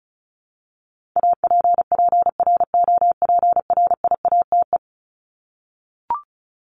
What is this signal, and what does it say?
Morse code, heard as a keyed tone, about 34 wpm, 700 Hz. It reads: APPROPRIATE
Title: Top 751-1000 Words 35wpm